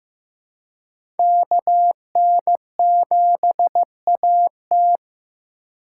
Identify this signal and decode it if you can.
Morse code, heard as a keyed tone, about 15 wpm, 700 Hz. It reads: KN7AT
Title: USA Call Signs - 5 Characters - 2x Character Spacing - 15wpm